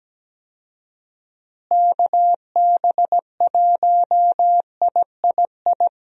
Morse code, heard as a keyed tone, about 17 wpm, 700 Hz. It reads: KB1III
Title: USA Call Signs - 6 Characters - Repeated 3x - 17wpm